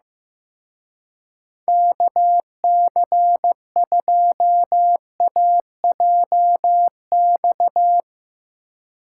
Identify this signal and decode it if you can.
Morse code, heard as a keyed tone, about 17 wpm, 700 Hz. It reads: KC2AJX
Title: Speed Racing - USA 2x3 Call Signs 15wpm